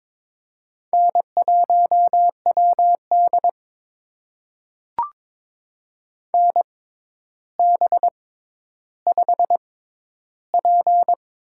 Morse code, heard as a keyed tone, about 22 wpm, spaced as for 11 wpm, 700 Hz. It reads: N1WD NB5P
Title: USA Call Signs - 4 Characters - 6x Character Spacing - 22wpm